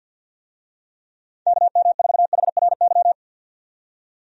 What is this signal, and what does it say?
Morse code, heard as a keyed tone, about 48 wpm, 700 Hz. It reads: KM4HRY
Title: USA Call Signs - 6 Characters - 50wpm